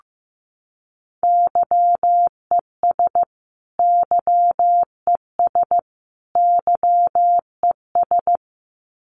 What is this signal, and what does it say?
Morse code, heard as a keyed tone, about 15 wpm, 700 Hz. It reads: YES YES YES
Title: Top 500 Words Repeated 15wpm